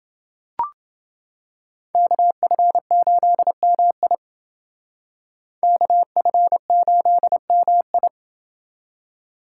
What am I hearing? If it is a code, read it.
Morse code, heard as a keyed tone, about 28 wpm, 700 Hz. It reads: KF8MI KF8MI